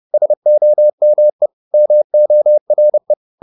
Morse code, heard as a keyed tone, about 30 wpm, 600 Hz. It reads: SOME MORE